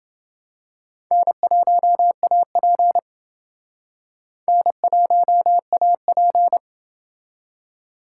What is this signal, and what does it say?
Morse code, heard as a keyed tone, about 28 wpm, 700 Hz. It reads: N1AP N1AP